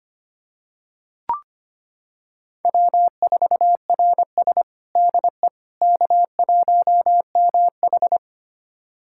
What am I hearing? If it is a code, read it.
Morse code, heard as a keyed tone, about 25 wpm, 700 Hz. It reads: W4RS DE K1MH
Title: USA Callsign DE Callsign - 4 Characters - 25wpm